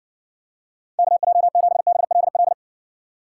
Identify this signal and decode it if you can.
Morse code, heard as a keyed tone, about 54 wpm, 700 Hz. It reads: KJ7BLL